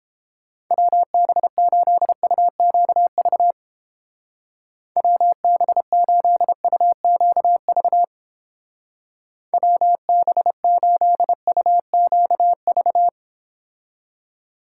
Morse code, heard as a keyed tone, about 30 wpm, 700 Hz. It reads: WB8UQV WB8UQV WB8UQV